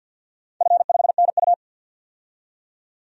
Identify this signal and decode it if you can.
Morse code, heard as a keyed tone, about 52 wpm, 700 Hz. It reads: F5NU
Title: DXCC-100 Call Signs - 4 Characters - 50wpm